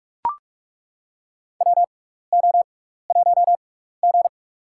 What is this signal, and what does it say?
Morse code, heard as a keyed tone, about 42 wpm, spaced as for 15 wpm, 700 Hz. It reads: WO1G
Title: USA Call Signs - 4 Characters - 6x Character Spacing - 45wpm